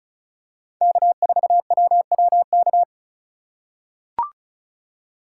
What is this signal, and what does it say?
Morse code, heard as a keyed tone, about 36 wpm, 700 Hz. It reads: K4WWK
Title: USA Call Signs - 5 Characters - 2x Character Spacing - 35wpm